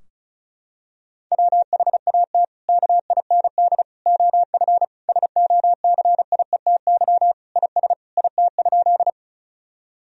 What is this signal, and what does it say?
Morse code, heard as a keyed tone, about 36 wpm, 700 Hz. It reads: WHAT KIND OF SOCIETY IS IT?
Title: Sentences from Top 1000 Words - Review Part 4 - 35wpm